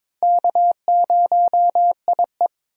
Morse code, heard as a keyed tone, about 22 wpm, 700 Hz. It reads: K0IE
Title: USA Call Signs - 4 Characters - 22wpm